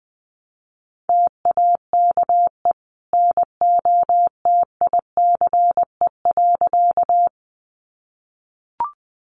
Morse code, heard as a keyed tone, about 20 wpm, 700 Hz. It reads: TAKE NOTICE.
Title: Sentences from Top 400 Words 20wpm